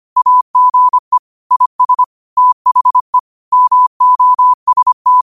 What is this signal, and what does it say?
Morse code, heard as a keyed tone, about 25 wpm, 1000 Hz. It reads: AGE IS THE MOST
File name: qotdfiles / 25wpm